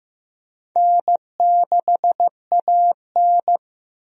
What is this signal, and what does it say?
Morse code, heard as a keyed tone, about 15 wpm, 700 Hz. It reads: N6AN
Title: USA Call Signs - 4 Characters - Repeated 3x - 15wpm